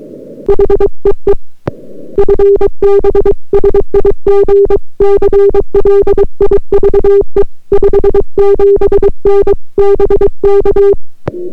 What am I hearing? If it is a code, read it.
Morse code, heard as a keyed tone, about 23 wpm, 400 Hz. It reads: HEE FB SIG CLIVE 57N BK